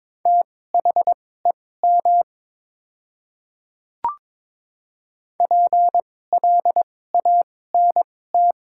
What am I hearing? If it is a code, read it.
Morse code, heard as a keyed tone, about 22 wpm, spaced as for 16 wpm, 700 Hz. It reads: THEM PLANT
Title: 2x Character Spacing - Top 300 Words - 22wpm